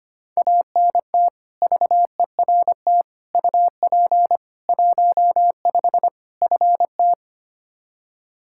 Morse code, heard as a keyed tone, about 25 wpm, 700 Hz. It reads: ANT VERT UP 15 FT
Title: Practice QSO - Exchange 2 - Info - 25wpm